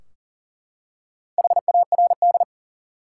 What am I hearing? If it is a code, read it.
Morse code, heard as a keyed tone, about 40 wpm, 700 Hz. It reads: HARD
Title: Four Letter Words - 40wpm